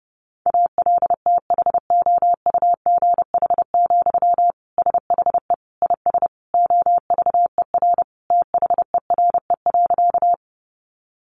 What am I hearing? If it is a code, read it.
Morse code, heard as a keyed tone, about 30 wpm, 700 Hz. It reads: ALTHOUGH, SHE IS OVER THERE.
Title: Sentences from Top 600 Words 30wpm